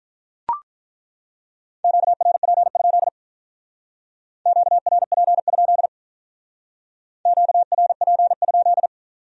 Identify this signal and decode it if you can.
Morse code, heard as a keyed tone, about 42 wpm, 700 Hz. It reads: QRP? QRP? QRP?
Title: Speed-Racing Q-Codes - 35wpm